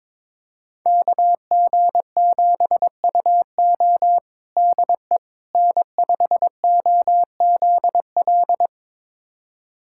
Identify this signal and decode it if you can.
Morse code, heard as a keyed tone, about 22 wpm, 700 Hz. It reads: KG7UO DE N5OZL
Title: USA Callsign DE Callsign - 5 Characters - 22wpm